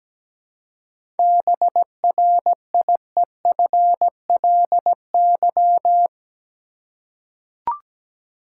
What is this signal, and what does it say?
Morse code, heard as a keyed tone, about 17 wpm, 700 Hz. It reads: BRIEFLY
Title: Seven Letter Words - 17wpm